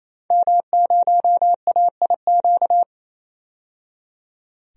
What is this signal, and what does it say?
Morse code, heard as a keyed tone, about 28 wpm, 700 Hz. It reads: M0AIQ
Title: DXCC-100 Call Signs - 5 Characters - 28wpm